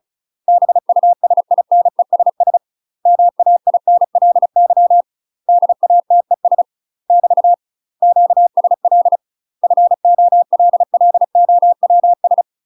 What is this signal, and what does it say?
Morse code, heard as a keyed tone, about 36 wpm, 700 Hz. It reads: BUSINESS MAINLY DATES <BT> QSL FOLLOWS
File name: UK MASTER QSOs TRAINING / 35WPM / UK MASTER QSOs 0020